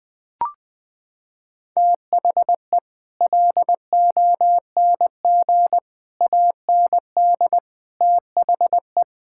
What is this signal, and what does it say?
Morse code, heard as a keyed tone, about 20 wpm, 700 Hz. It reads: THE LONG AND THE
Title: Sets of 9 Words - 20wpm